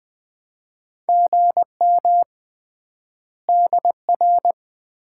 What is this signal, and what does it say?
Morse code, heard as a keyed tone, about 20 wpm, 700 Hz. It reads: GM DR
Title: Common QSO Phrases - 3x Word-Spacing - 20wpm